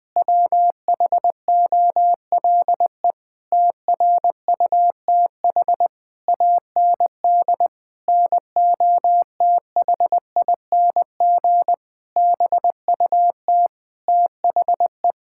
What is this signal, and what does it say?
Morse code, heard as a keyed tone, about 20 wpm, 700 Hz. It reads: WHOLE TRUTH AND NOTHING BUT THE
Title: Sentences with 8-Word Phrases - 20wpm